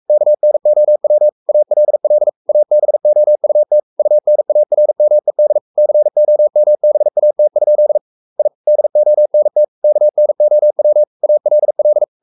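Morse code, heard as a keyed tone, about 44 wpm, 600 Hz. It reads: KNOW ALL ABOUT UNARMED COMBAT? I DONT KNOW ALL
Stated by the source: Morse code at 43 wpm sending Ken Follett's novel Never in English, from CW med Gnister.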